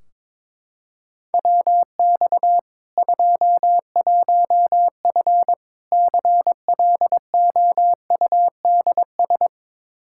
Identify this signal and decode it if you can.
Morse code, heard as a keyed tone, about 22 wpm, 700 Hz. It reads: WX 21F CLOUDS
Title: K1USN SST QSO Practice 22wpm